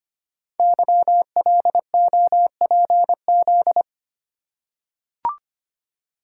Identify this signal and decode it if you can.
Morse code, heard as a keyed tone, about 25 wpm, 700 Hz. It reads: YLOPZ
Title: Code Groups - 5 Letters - 25wpm